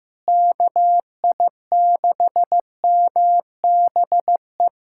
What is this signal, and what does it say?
Morse code, heard as a keyed tone, about 16 wpm, 700 Hz. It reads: KI6MBE
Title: USA Callsign DE Callsign - 6 Characters - 15wpm